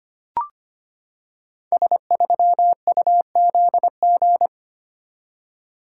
Morse code, heard as a keyed tone, about 25 wpm, 700 Hz. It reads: S3UZG
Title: Code Groups - 5 Letter:Numbers - 25wpm